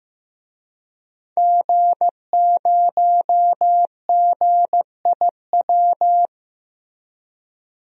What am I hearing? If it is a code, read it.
Morse code, heard as a keyed tone, about 15 wpm, 700 Hz. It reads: G0GIW
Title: DXCC-100 Call Signs - 5 Characters - Repeated 3x - 15wpm